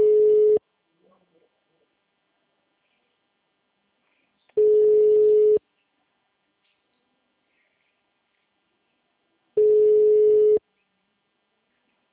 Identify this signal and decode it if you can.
Ringback tone (Europe, 425 Hz), with speech.
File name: out- -2003- - - .64